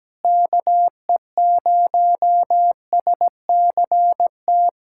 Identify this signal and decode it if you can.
Morse code, heard as a keyed tone, about 17 wpm, 700 Hz. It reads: KE0SCT
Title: USA Callsign DE Callsign - 6 Characters - 17wpm